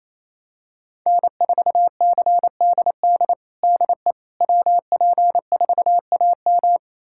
Morse code, heard as a keyed tone, about 27 wpm, 700 Hz. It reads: N4CDD DE WP4AM
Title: USA Callsign DE Callsign - 5 Characters - 28wpm